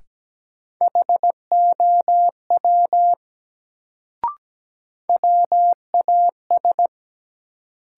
Morse code, heard as a keyed tone, about 17 wpm, 700 Hz. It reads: HOW WAS
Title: Sentences from Top 100 Words - New Word Drill - 17wpm